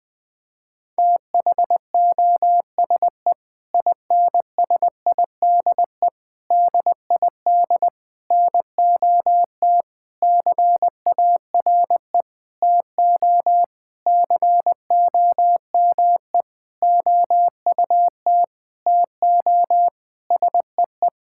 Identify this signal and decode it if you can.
Morse code, heard as a keyed tone, about 21 wpm, 700 Hz. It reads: THOSE INSIDE DID NOT CARE TO COME OUT TO SEE
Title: Sentences from Top 1000 Words - Review Part 6 - 20wpm